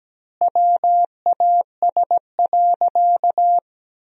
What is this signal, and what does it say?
Morse code, heard as a keyed tone, about 17 wpm, 700 Hz. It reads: WAS.
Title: 4x Word-Spacing - Sentences from Top 100 Words - 17wpm